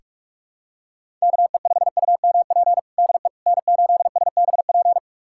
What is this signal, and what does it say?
Morse code, heard as a keyed tone, about 42 wpm, 700 Hz. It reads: KE5UMP DE N8SBP
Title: USA Callsign DE Callsign 45wpm